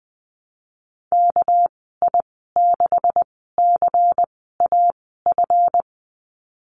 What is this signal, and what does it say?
Morse code, heard as a keyed tone, about 20 wpm, spaced as for 15 wpm, 700 Hz. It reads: KI6CAF
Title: 213 Lesson 34g - Number 8 - Call Signs 20wpm